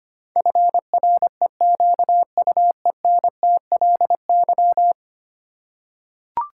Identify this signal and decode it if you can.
Morse code, heard as a keyed tone, about 25 wpm, 700 Hz. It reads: FREQUENTLY
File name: Ten Letter Words - 25wpm